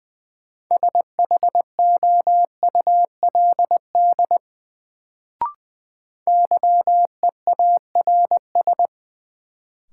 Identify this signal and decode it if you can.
Morse code, heard as a keyed tone, about 21 wpm, 700 Hz. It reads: SHOULD YEARS
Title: Sets of 2 Words - New Word Drill - 20wpm